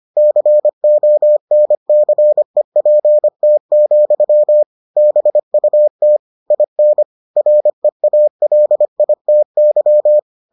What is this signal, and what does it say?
Morse code, heard as a keyed tone, about 25 wpm, 600 Hz. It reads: CONCEPT, BUT IN REALITY